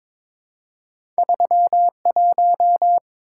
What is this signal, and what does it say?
Morse code, heard as a keyed tone, about 22 wpm, 700 Hz. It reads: 31
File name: practice / 22wpm / Two Letter-Number Combinations - 22wpm